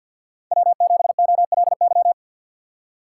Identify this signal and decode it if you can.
Morse code, heard as a keyed tone, about 53 wpm, 700 Hz. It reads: W7OLY
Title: USA Call Signs - 5 Characters - 3x Character Spacing - 50wpm